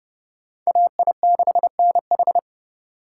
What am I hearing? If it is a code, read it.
Morse code, heard as a keyed tone, about 26 wpm, 700 Hz. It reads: AI6NH